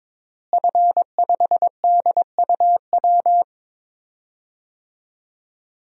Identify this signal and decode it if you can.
Morse code, heard as a keyed tone, about 22 wpm, 700 Hz. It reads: F5DUW
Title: DXCC-100 Call Signs - 5 Characters - Repeated 2x - 22wpm